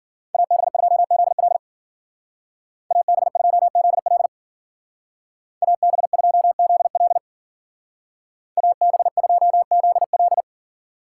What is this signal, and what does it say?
Morse code, heard as a keyed tone, about 49 wpm, 700 Hz. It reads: AB2ZL AB2ZL AB2ZL AB2ZL
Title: Speed Racing - USA 2x2 Call Signs 40wpm